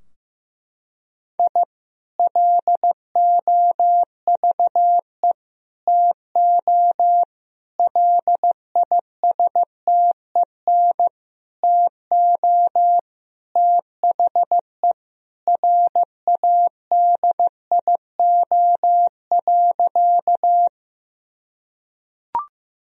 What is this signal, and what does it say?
Morse code, heard as a keyed tone, about 15 wpm, 700 Hz. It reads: I LOVE TO LISTEN TO THE RADIO.
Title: Sentences from Top 1000 Words - Review Part 3 - 15wpm